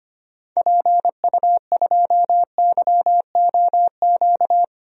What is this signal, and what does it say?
Morse code, heard as a keyed tone, about 25 wpm, 690 Hz. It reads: PU2YOQ
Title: DXCC-100 Call Signs - 6 Characters - 25wpm